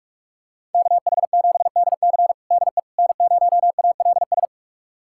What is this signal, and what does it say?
Morse code, heard as a keyed tone, about 46 wpm, 700 Hz. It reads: KH7DC DE N0ALS
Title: USA Callsign DE Callsign - 5 Characters - 45wpm